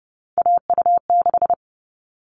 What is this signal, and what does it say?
Morse code, heard as a keyed tone, about 30 wpm, 700 Hz. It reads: AU6
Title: All Cut Numbers - 3-Digit 30wpm